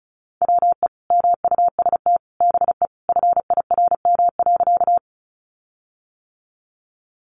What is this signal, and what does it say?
Morse code, heard as a keyed tone, about 34 wpm, 700 Hz. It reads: WE MUST BE FIRM.
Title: Sentences from Top 1400 Words 35wpm